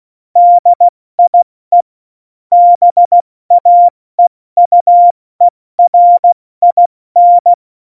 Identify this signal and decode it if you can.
Morse code, heard as a keyed tone, about 16 wpm, 700 Hz. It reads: DIE BAEUERIN
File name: Morsekurs / Witze / Witz 21 Juli 2020, 70 F80